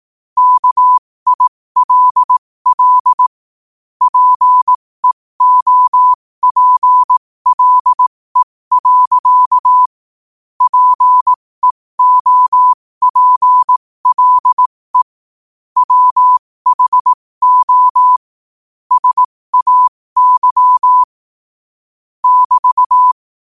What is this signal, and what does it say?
Morse code, heard as a keyed tone, about 18 wpm, spaced as for 15 wpm, 1000 Hz. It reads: KILL PEOPLE. PEOPLE WHO SAY <BT>
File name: qotdfiles / 15wpm